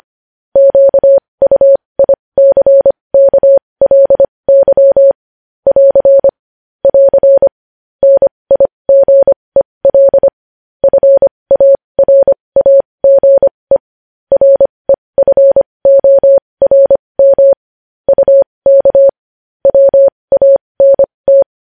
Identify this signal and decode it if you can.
Morse code, heard as a keyed tone, about 25 wpm, spaced as for 20 wpm, 550 Hz. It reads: QUICKLY <AR> <AR> NIGEL FARAGE REFORM UK WANT
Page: News Headlines in Morse Code at 20 WPM